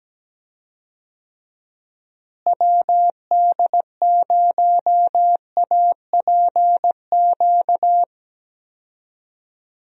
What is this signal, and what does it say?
Morse code, heard as a keyed tone, about 17 wpm, 700 Hz. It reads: WD0APQ
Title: USA Call Signs - 6 Characters - Repeated 3x - 17wpm